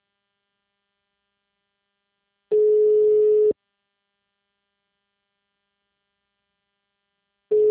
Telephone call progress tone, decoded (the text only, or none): ringback tone